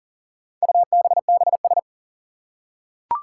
Morse code, heard as a keyed tone, about 40 wpm, 700 Hz. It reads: UBBS
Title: Four Challenging Letter-Number Combinations 40wpm